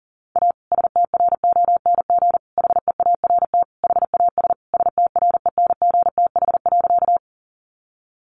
Morse code, heard as a keyed tone, about 40 wpm, 700 Hz. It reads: A STRONG HEART HAS STRENGTH.
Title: Sentences from Top 1300 Words 40wpm